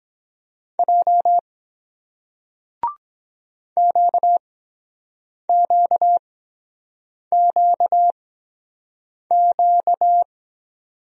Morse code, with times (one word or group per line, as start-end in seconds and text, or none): 0.8-1.4 J
3.8-10.2 QQQQ